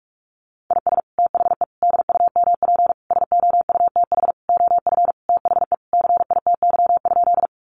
Morse code, heard as a keyed tone, about 46 wpm, 700 Hz. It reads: IS THE DUMP SOUTH OF THE CITY?
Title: Sentences from Top 1300 Words 45wpm